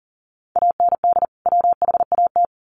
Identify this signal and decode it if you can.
Morse code, heard as a keyed tone, about 40 wpm, 700 Hz. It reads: AND WHAT
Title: Sentences from Top 1500 Words 40wpm